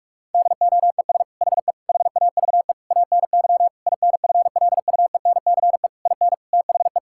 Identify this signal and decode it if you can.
Morse code, heard as a keyed tone, about 47 wpm, 700 Hz. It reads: DOES HE HAVE ANY INFLUENCE IN THE
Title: Sentences from Top 1600 Words 45wpm